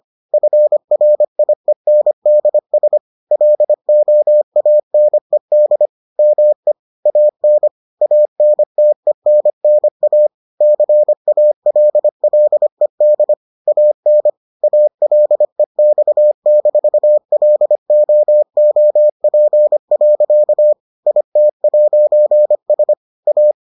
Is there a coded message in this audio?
Yes: Morse code from 0.3 to 23.5 s, heard as a keyed tone, about 25 wpm, 600 Hz. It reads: FRIENDS LOANED ME AN ANTENNA CALLED AN ALEX-LOOP. IT'S A